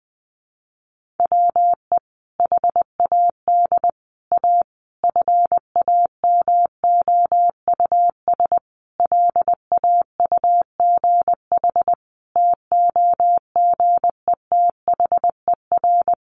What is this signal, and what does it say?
Morse code, heard as a keyed tone, about 19 wpm, 700 Hz. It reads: WE HAD A FAMOUS LAUGH TOGETHER
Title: Sentences from Top 1500 Words 20wpm